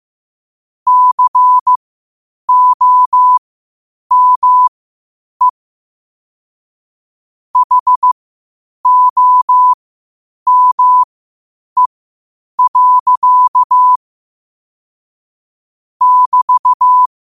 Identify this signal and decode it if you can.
Morse code, heard as a keyed tone, about 15 wpm, spaced as for 8 wpm, 1000 Hz. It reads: COME HOME. <BT>
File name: qotdfiles / 8wpm